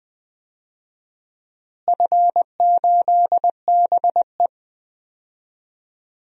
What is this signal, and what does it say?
Morse code, heard as a keyed tone, about 20 wpm, 700 Hz. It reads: F8BE
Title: DXCC-100 Call Signs - 4 Characters - Repeated 2x - 20wpm